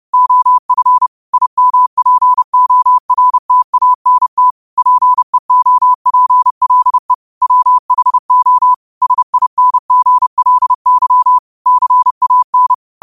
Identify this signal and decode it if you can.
Morse code, heard as a keyed tone, about 30 wpm, 1000 Hz. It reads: OF IMPORTANT PEOPLE WHO SINGLY CAN
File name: qotdfiles / 30wpm